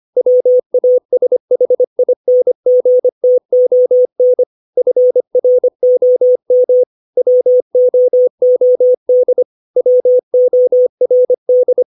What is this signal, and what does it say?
Morse code, heard as a keyed tone, about 25 wpm, 500 Hz. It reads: WASHINGTON FROM WOOD WORD
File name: 632 words / 632 Words 25WPM